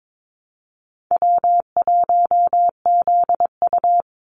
Morse code, heard as a keyed tone, about 22 wpm, 700 Hz. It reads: W1ZU
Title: NA SOTA Activator Callsigns 22wpm